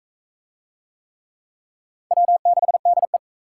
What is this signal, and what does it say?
Morse code, heard as a keyed tone, about 40 wpm, 700 Hz. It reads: W6DE